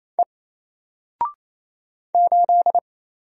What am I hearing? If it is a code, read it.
Morse code, heard as a keyed tone, about 27 wpm, 700 Hz. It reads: E 8
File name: Single-Letter-Number-28wpm